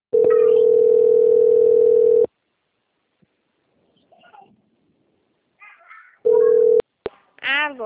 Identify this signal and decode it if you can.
Ringback tone (North America), 440 + 480 Hz.